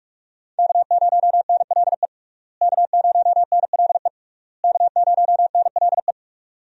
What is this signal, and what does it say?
Morse code, heard as a keyed tone, about 46 wpm, 700 Hz. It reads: K0NLE K0NLE K0NLE
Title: USA Call Signs - 5 Characters - Repeated 3x - 45wpm